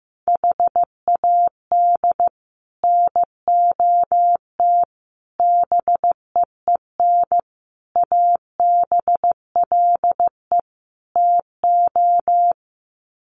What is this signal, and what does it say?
Morse code, heard as a keyed tone, about 15 wpm, 700 Hz. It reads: HAD NOT BEEN ABLE TO
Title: Sets of 5 Words Encore 15wpm